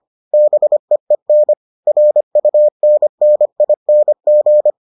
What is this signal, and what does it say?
Morse code, heard as a keyed tone, about 25 wpm, 600 Hz. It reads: BEEN RUNNING